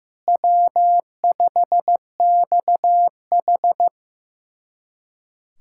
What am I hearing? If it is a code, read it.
Morse code, heard as a keyed tone, about 15 wpm, 700 Hz. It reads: W5XH
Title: USA Callsign DE Callsign - 4 Characters - 15wpm